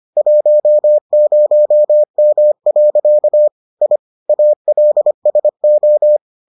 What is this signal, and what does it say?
Morse code, heard as a keyed tone, about 25 wpm, 600 Hz. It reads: 10M. I ALSO